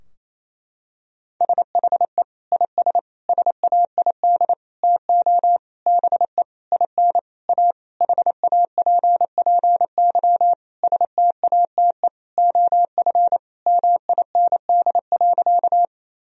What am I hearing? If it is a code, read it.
Morse code, heard as a keyed tone, about 28 wpm, 700 Hz. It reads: SHE IS SAID TO BE IN A HAPPY STATE OF MIND.
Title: Sentences from Top 1000 Words - Review Part 2 - 28wpm